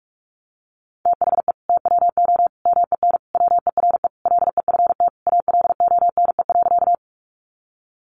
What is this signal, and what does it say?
Morse code, heard as a keyed tone, about 44 wpm, 700 Hz. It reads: THE TWO MEN WERE LEFT ALONE.
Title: Sentences from Top 1000 Words 45wpm